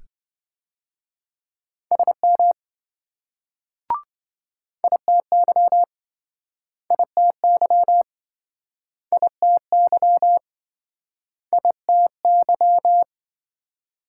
Morse code, heard as a keyed tone, about 26 wpm, 700 Hz. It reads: SM ITY ITY ITY ITY